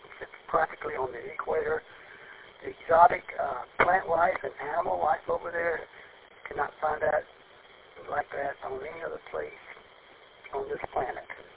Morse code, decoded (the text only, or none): none